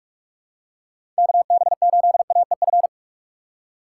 Morse code, heard as a keyed tone, about 46 wpm, 700 Hz. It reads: KB8AEF
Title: USA Call Signs - 6 Characters - 2x Character Spacing - 45wpm